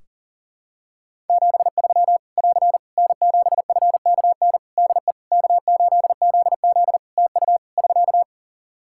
Morse code, heard as a keyed tone, about 40 wpm, 700 Hz. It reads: 73 <AR> N7FKN DE K8ZZ TU <SK>
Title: Practice QSO - Exchange 4 - Ending - 40wpm